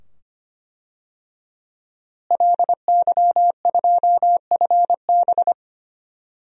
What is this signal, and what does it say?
Morse code, heard as a keyed tone, about 25 wpm, 700 Hz. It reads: LY2FB
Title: DXCC-100 Call Signs - 5 Characters - Repeated 2x - 25wpm